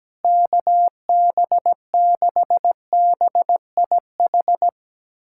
Morse code, heard as a keyed tone, about 17 wpm, 700 Hz. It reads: KB6BIH
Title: USA Callsign DE Callsign - 6 Characters - 17wpm